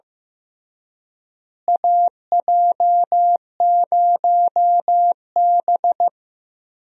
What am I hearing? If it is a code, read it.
Morse code, heard as a keyed tone, about 15 wpm, 700 Hz. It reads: AJ0B
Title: USA Call Signs - 4 Characters - Repeated 3x - 15wpm